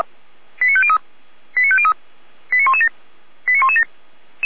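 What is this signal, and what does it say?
Five-tone selective call (ZVEI), heard as quick runs of five tones, tones of about 70 ms. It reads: 89503 89503 89117 89117
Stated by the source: ZVEI five-tone call 89092 received on an SDR, channel 455.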